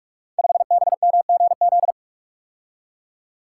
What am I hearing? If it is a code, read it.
Morse code, heard as a keyed tone, about 44 wpm, 700 Hz. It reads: 5BMGZ
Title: Code Groups - 5 Letter:Numbers - 45wpm